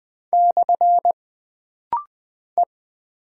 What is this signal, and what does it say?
Morse code, heard as a keyed tone, about 20 wpm, 700 Hz. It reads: / E
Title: Single Character (Letter:Number:Punctuation:Prosign) - Rapid-Fire - 20wpm